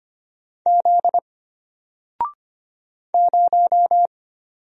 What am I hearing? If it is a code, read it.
Morse code, heard as a keyed tone, about 25 wpm, 700 Hz. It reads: Z 0
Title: Single Letter-Number Alt 25wpm